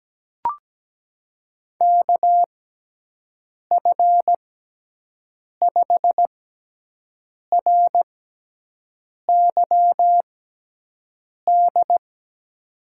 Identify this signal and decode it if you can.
Morse code, heard as a keyed tone, about 17 wpm, spaced as for 6 wpm, 700 Hz. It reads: KF5RYD
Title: USA Call Signs - 6 Characters - 6x Character Spacing - 17wpm